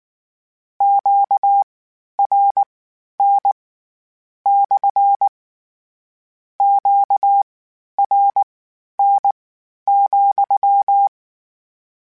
Morse code, heard as a keyed tone, about 19 wpm, spaced as for 11 wpm, 800 Hz. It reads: QRN / QRN,
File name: Q Calls / Q Calls-mp3s / Q Calls SPSL2 @18